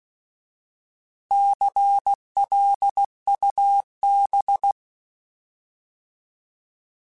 Morse code, heard as a keyed tone, about 16 wpm, 800 Hz. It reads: CLUB
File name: Alphabetical / Alphabetical mp3s / Cs @15